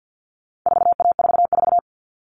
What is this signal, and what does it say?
Morse code, heard as a keyed tone, about 50 wpm, 700 Hz. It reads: 4A44
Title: All Cut Numbers - 4-Digit 50wpm